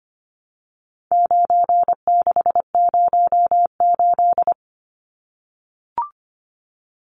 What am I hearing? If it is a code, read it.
Morse code, heard as a keyed tone, about 25 wpm, 700 Hz. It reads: 9608
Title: Numbers - 4-Digit 25wpm